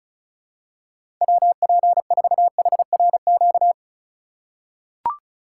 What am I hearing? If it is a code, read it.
Morse code, heard as a keyed tone, about 36 wpm, 700 Hz. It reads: WP4HRQ
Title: USA Call Signs - 6 Characters - 4x Character Spacing - 35wpm